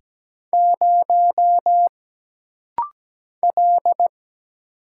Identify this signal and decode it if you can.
Morse code, heard as a keyed tone, about 17 wpm, 700 Hz. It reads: 0 L
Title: Single Letter-Number - Rapid-Fire 17wpm